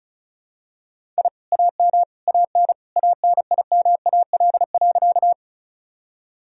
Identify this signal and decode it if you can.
Morse code, heard as a keyed tone, about 34 wpm, 700 Hz. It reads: I AM AN ANIMAL.
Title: Sentences from Top 1000 Words - Review Part 1 - 35wpm